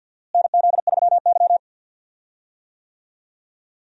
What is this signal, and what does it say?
Morse code, heard as a keyed tone, about 47 wpm, 700 Hz. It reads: NZ3Y
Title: USA Call Signs - 4 Characters - Repeated 2x - 50wpm